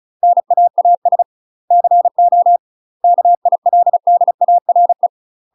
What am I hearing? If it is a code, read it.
Morse code, heard as a keyed tone, about 34 wpm, 700 Hz. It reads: NAAS CO KILDARE
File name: UK MASTER QSOs TRAINING / 35WPM / UK MASTER QSOs 0000